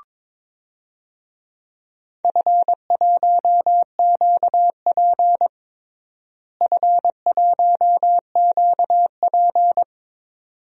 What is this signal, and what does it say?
Morse code, heard as a keyed tone, about 22 wpm, 700 Hz. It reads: F1QP F1QP
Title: DXCC-100 Call Signs - 4 Characters - Repeated 2x - 22wpm